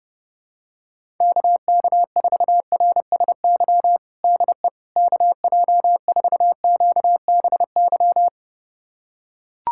Morse code, heard as a keyed tone, about 30 wpm, 700 Hz. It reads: KK4RSY DE KJ4QBY
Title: USA Callsign DE Callsign - 6 Characters - 30wpm